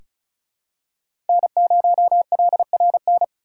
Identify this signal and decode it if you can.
Morse code, heard as a keyed tone, about 34 wpm, 700 Hz. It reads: N0LRN
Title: USA Call Signs - 6x Character Spacing - 35wpm